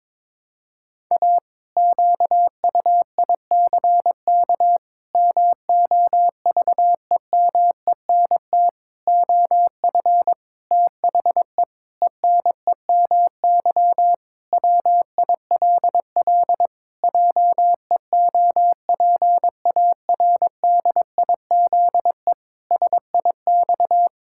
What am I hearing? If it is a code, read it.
Morse code, heard as a keyed tone, about 22 wpm, 700 Hz. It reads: A QUICK MOVEMENT OF THE ENEMY WILL JEOPARDIZE SIX
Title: Pangram Challenge v2 - 22wpm